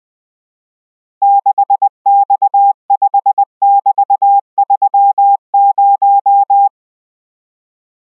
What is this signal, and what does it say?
Morse code, heard as a keyed tone, about 20 wpm, 800 Hz. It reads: 6X5<BT>30